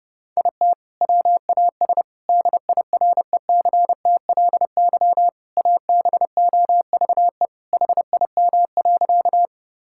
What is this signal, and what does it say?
Morse code, heard as a keyed tone, about 30 wpm, 700 Hz. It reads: IT WAS DIRECTLY ABOVE HIM.
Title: Sentences from Top 1000 Words - Review Part 5 - 30wpm